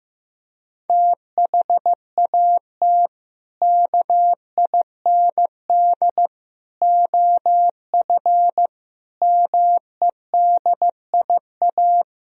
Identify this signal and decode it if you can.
Morse code, heard as a keyed tone, about 15 wpm, 700 Hz. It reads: THAT KIND OF MEDIA
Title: Sentences from Top 1000 Words - Review Part 4 - 15wpm